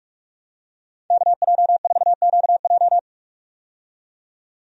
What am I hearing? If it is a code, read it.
Morse code, heard as a keyed tone, about 46 wpm, 700 Hz. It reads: KJ4QJ
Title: USA Call Signs - 5 Characters - 45wpm